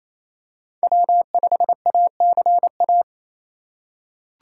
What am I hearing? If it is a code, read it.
Morse code, heard as a keyed tone, about 28 wpm, 700 Hz. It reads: W5ACA
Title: USA Call Signs - 5 Characters - 6x Character Spacing - 28wpm